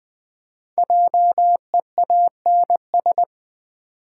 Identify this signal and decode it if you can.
Morse code, heard as a keyed tone, about 20 wpm, 700 Hz. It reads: JEANS
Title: Easy Words with Hints - 20wpm